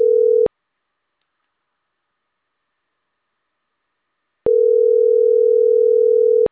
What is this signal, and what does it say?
Ringback tone (North America), 440 + 480 Hz.